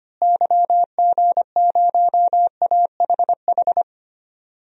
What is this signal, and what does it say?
Morse code, heard as a keyed tone, about 25 wpm, 700 Hz. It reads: YG0AHH
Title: DXCC-100 Call Signs - 6 Characters - Repeated 3x - 25wpm